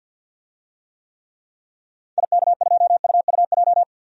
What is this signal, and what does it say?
Morse code, heard as a keyed tone, about 51 wpm, 700 Hz. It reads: IK2UUJ
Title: DXCC-100 Call Signs - 6 Characters - 50wpm